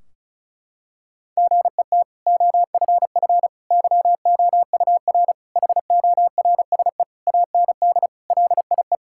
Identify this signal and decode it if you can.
Morse code, heard as a keyed tone, about 36 wpm, 700 Hz. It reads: GET OFF YOUR HORSE AND LIE
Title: Sentences from Top 1600 Words 35wpm